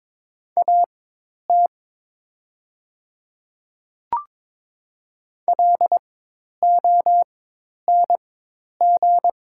Morse code, heard as a keyed tone, about 22 wpm, spaced as for 10 wpm, 700 Hz. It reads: AT LONG
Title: 4x Character Spacing - Top 100 Words - 22wpm